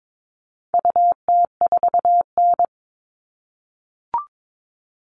Morse code, heard as a keyed tone, about 22 wpm, 700 Hz. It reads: UT4N
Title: All Cut Numbers - 4-Digit 22wpm